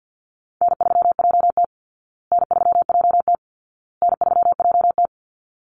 Morse code, heard as a keyed tone, about 48 wpm, 700 Hz. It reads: N3JT N3JT N3JT
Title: Easy Call Signs Repeated 3x - 50wpm